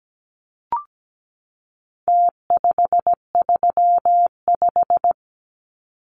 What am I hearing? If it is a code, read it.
Morse code, heard as a keyed tone, about 17 wpm, 700 Hz. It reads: T535
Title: Numbers - Contest Style - 4-Digit 17wpm